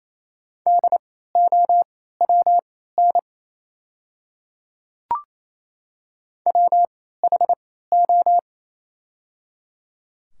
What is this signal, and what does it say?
Morse code, heard as a keyed tone, about 28 wpm, spaced as for 16 wpm, 700 Hz. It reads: DOWN WHO
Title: 3x Character Spacing - Top 100 Words - 28wpm